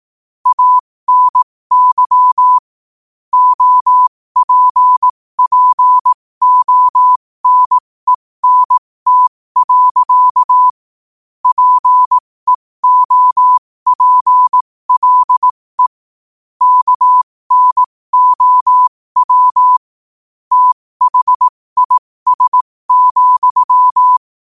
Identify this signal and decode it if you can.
Morse code, heard as a keyed tone, about 18 wpm, spaced as for 15 wpm, 1000 Hz. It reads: ANY OPPONENT. PEOPLE KNOW THIS,
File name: 15wpm